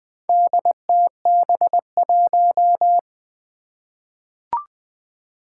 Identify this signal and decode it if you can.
Morse code, heard as a keyed tone, about 20 wpm, 700 Hz. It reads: DTB1
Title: Four Challenging Letter-Number Combinations 20wpm